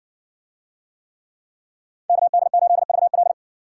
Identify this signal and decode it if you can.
Morse code, heard as a keyed tone, about 60 wpm, 700 Hz. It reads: KD8VL